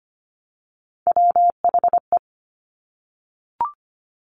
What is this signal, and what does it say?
Morse code, heard as a keyed tone, about 25 wpm, 700 Hz. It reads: WHE
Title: Top Three Letter Combinations in Words 25wpm